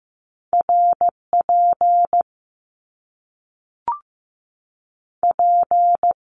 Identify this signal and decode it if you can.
Morse code, heard as a keyed tone, about 15 wpm, spaced as for 3 wpm, 700 Hz. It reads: R PP
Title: Top Two Letter Combinations in Words 15wpm